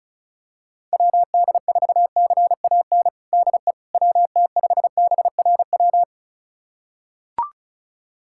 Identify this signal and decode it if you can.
Morse code, heard as a keyed tone, about 36 wpm, 700 Hz. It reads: WD4CAN DE WT5BRW
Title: USA Callsign DE Callsign 35wpm